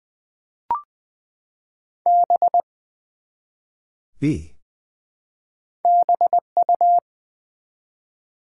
Morse code, heard as a keyed tone, about 20 wpm, 700 Hz. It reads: B BU
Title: Build A Word - 1 Letter At A Time - 3 to 5-Letter Words - 20wpm